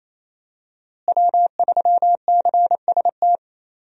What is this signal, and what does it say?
Morse code, heard as a keyed tone, about 28 wpm, 700 Hz. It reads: W3CST
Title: USA Call Signs - 5 Characters - Repeated 2x - 28wpm